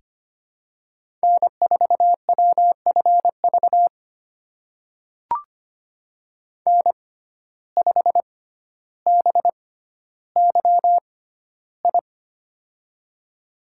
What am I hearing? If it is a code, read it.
Morse code, heard as a keyed tone, about 25 wpm, spaced as for 13 wpm, 690 Hz. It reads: N4WFV N5BYI
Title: USA Call Signs - 5 Characters - 6x Character Spacing - 25wpm